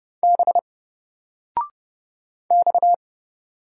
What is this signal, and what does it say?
Morse code, heard as a keyed tone, about 30 wpm, 700 Hz. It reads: B X
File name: Single-Letter-Number-30wpm